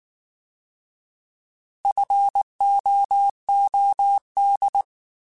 Morse code, heard as a keyed tone, about 19 wpm, 800 Hz. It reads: FOOD